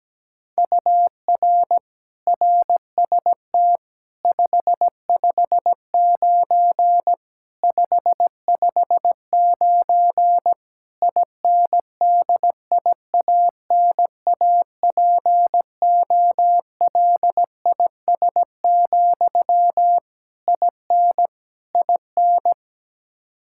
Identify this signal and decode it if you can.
Morse code, heard as a keyed tone, about 17 wpm, 700 Hz. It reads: UR RST 559 559 INDIANAPOLIS, IN IN
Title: Practice QSO - Exchange 1 - Beginning - 17wpm